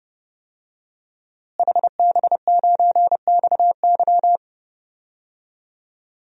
Morse code, heard as a keyed tone, about 30 wpm, 700 Hz. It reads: HB9XY
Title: DXCC-100 Call Signs - 5 Characters - Repeated 3x - 30wpm